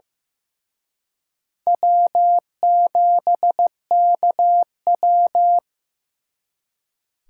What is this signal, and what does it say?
Morse code, heard as a keyed tone, about 18 wpm, 700 Hz. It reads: W7KW
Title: Speed Racing - USA 1x2 Call Signs 15wpm